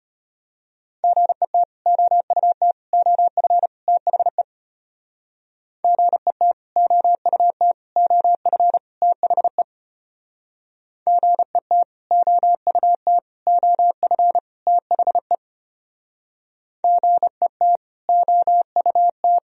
Morse code, heard as a keyed tone, about 32 wpm, 700 Hz. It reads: GET OUT OF THE GET OUT OF THE GET OUT OF THE GET OUT